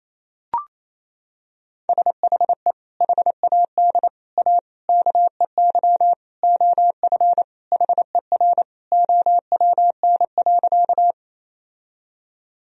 Morse code, heard as a keyed tone, about 27 wpm, 700 Hz. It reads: SHE HAD A KEY OF HER OWN.
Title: Sentences from Top 1000 Words - Review Part 4 - 28wpm